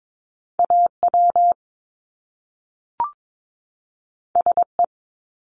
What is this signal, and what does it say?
Morse code, heard as a keyed tone, about 22 wpm, 700 Hz. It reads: AW SE
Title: Top Two Letter Combinations in Words 22wpm